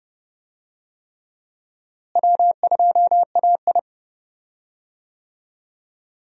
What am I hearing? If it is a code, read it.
Morse code, heard as a keyed tone, about 30 wpm, 700 Hz. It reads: W2AI
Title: USA Call Signs - 4 Characters - Repeated 3x - 30wpm